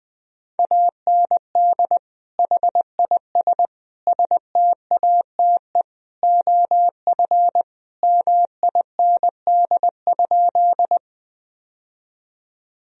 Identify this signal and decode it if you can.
Morse code, heard as a keyed tone, about 20 wpm, 700 Hz. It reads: AND HIS STATE OF MIND?
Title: Sentences from Top 1000 Words - Review Part 2 - 20wpm